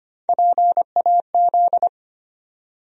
Morse code, heard as a keyed tone, about 25 wpm, 700 Hz. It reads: PAZ
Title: USA Call Sign Postfixes - 3 Letters 25wpm